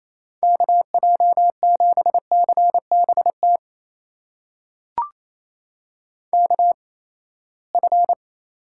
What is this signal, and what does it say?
Morse code, heard as a keyed tone, about 27 wpm, spaced as for 19 wpm, 700 Hz. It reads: KJ7CBT KF
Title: USA Call Signs - 6 Characters - 8x Character Spacing - 28wpm